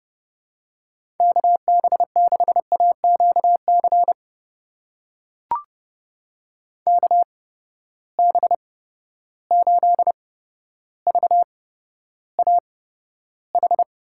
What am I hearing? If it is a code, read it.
Morse code, heard as a keyed tone, about 31 wpm, spaced as for 13 wpm, 700 Hz. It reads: KB6AQC KB8VAH
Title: USA Call Signs - 6 Characters - 8x Character Spacing - 30wpm